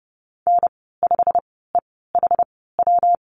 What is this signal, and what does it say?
Morse code, heard as a keyed tone, about 30 wpm, spaced as for 17 wpm, 700 Hz. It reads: N5EHW
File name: Lesson 19g - Letter w - Call Signs 30wpm